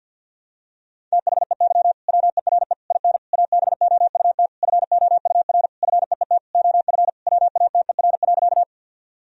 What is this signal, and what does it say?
Morse code, heard as a keyed tone, about 46 wpm, 700 Hz. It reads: THEY WERE IN ABOUT FOUR FEET OF WATER.
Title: Sentences from Top 1000 Words - Review Part 1 - 50wpm